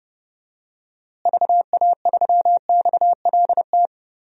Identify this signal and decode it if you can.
Morse code, heard as a keyed tone, about 32 wpm, 700 Hz. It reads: VA3XLT
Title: DXCC-100 Call Signs - 6 Characters - Repeated 3x - 30wpm